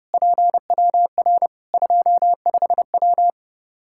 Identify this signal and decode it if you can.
Morse code, heard as a keyed tone, about 30 wpm, 700 Hz. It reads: PWR 25W
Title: K1USN SST QSO Practice 30wpm